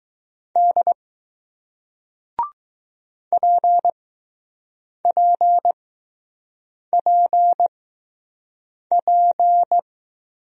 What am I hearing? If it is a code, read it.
Morse code, heard as a keyed tone, about 20 wpm, spaced as for 6 wpm, 700 Hz. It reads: D PPPP